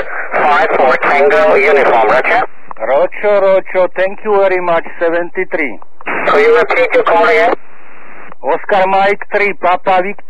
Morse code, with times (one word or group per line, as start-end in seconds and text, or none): none